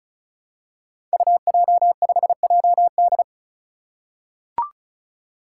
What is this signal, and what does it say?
Morse code, heard as a keyed tone, about 36 wpm, 700 Hz. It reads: UJ5JD
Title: Code Groups - 5 Letter:Numbers - 35wpm